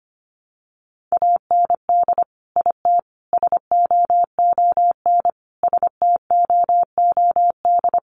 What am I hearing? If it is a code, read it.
Morse code, heard as a keyed tone, about 25 wpm, 700 Hz. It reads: AND IT SOON STOOD
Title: Sentences from Top 900 Words 25wpm